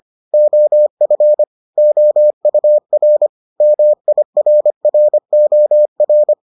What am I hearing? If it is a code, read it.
Morse code, heard as a keyed tone, about 25 wpm, 600 Hz. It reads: OF OUR MIRROR